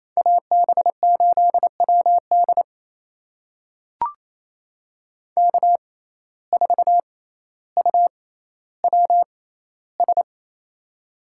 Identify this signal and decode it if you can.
Morse code, heard as a keyed tone, about 28 wpm, spaced as for 14 wpm, 700 Hz. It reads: AB8WD K4UWS
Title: USA Call Signs - 5 Characters - 6x Character Spacing - 28wpm